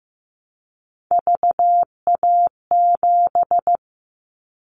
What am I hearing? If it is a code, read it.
Morse code, heard as a keyed tone, about 15 wpm, 700 Hz. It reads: VA7
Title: Callsign Prefixes for Top 50 DXCC - 15wpm